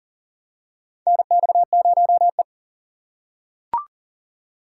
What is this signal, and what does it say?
Morse code, heard as a keyed tone, about 40 wpm, 700 Hz. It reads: NX0E
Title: USA Call Signs - 4 Characters - 6x Character Spacing - 40wpm